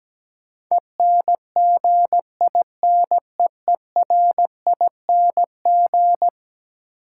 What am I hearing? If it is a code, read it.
Morse code, heard as a keyed tone, about 17 wpm, 700 Hz. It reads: ENGINEERING
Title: Sentences from Top 1500 Words - New Word Drill Alt - 17wpm